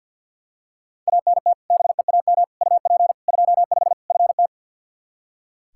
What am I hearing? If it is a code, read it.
Morse code, heard as a keyed tone, about 46 wpm, 700 Hz. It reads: ANT BEAM UP 25 FT
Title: K1USN SST QSO Practice 50wpm